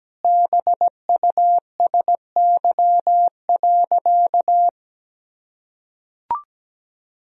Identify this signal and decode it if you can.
Morse code, heard as a keyed tone, about 17 wpm, 700 Hz. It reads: BUSY.
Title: Sentences from Top 1000 Words - Review Part 3 - 17wpm